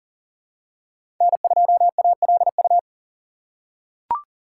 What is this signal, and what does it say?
Morse code, heard as a keyed tone, about 40 wpm, 700 Hz. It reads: N2ALU
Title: USA Call Signs - 3x Character Spacing - 40wpm